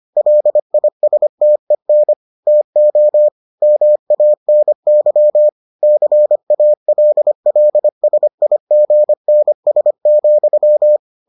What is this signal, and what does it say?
Morse code, heard as a keyed tone, about 25 wpm, 600 Hz. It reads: LISTEN TO MANY CALLSIGNS,